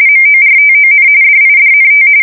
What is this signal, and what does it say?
RTTY, about 45.45 baud. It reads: MORTON DIDNT
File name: fsk170 / teletype-corp